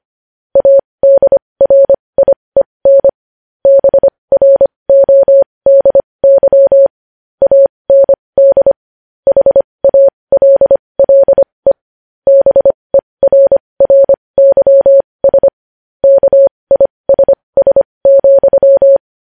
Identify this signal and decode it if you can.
Morse code, heard as a keyed tone, about 25 wpm, spaced as for 20 wpm, 550 Hz. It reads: ADRIEN BRODY AND HALLE BERRYS KISS,
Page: news-headlines-in-morse-code-at-20-wpm